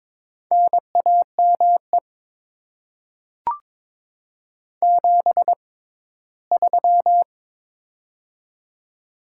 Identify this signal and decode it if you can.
Morse code, heard as a keyed tone, about 22 wpm, spaced as for 15 wpm, 700 Hz. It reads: NAME 73
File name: Top 20 QSO Elements - 6x Character Spacing - 22wpm